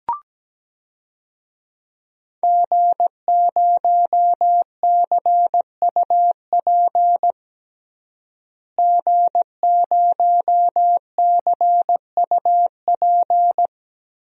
Morse code, heard as a keyed tone, about 17 wpm, 700 Hz. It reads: G0CUP G0CUP
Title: DXCC-100 Call Signs - 5 Characters - Repeated 3x - 17wpm